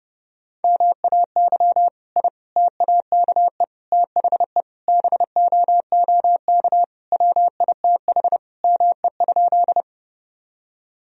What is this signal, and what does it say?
Morse code, heard as a keyed tone, about 30 wpm, 700 Hz. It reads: MAY I TAKE THE BOOK WITH ME?
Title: Sentences from Top 1000 Words - Review Part 1 - 30wpm